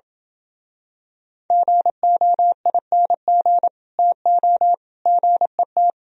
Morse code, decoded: GOING TO GET

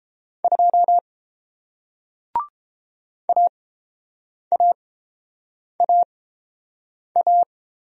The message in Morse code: 2 AAAA

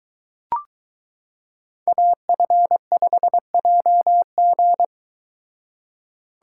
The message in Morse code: AF5JG

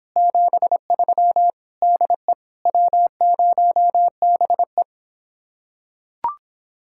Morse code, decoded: 73 DE W0BE

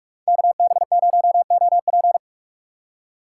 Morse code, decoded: KB0OP

